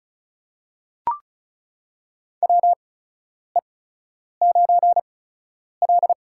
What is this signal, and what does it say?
Morse code, heard as a keyed tone, about 34 wpm, spaced as for 10 wpm, 700 Hz. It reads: WE9L
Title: USA Call Signs - 4 Characters - 8x Character Spacing - 35wpm